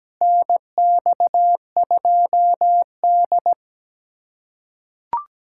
Morse code, heard as a keyed tone, about 17 wpm, 700 Hz. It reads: NX2D
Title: USA Callsign DE Callsign - 4 Characters - 17wpm